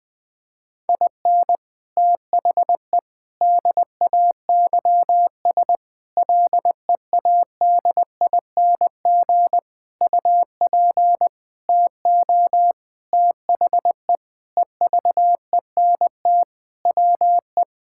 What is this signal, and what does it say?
Morse code, heard as a keyed tone, about 20 wpm, 700 Hz. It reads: IN THE DAYS LEADING UP TO THE EVENT WE
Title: Sentences with 6-Word Phrases - Spoken Beginning - 20wpm